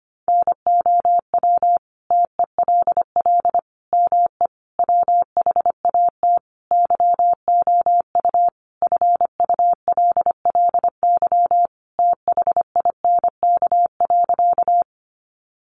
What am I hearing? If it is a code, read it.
Morse code, heard as a keyed tone, about 25 wpm, 700 Hz. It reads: NOW TELL ME WHAT YOU FULLY THINK.
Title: Sentences from Top 1300 Words 25wpm